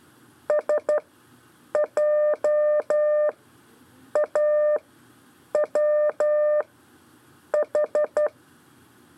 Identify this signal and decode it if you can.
Morse code, heard as a keyed tone, about 12 wpm, spaced as for 7 wpm, 600 Hz. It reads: SJAWH
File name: 4 LEZIONE CW 1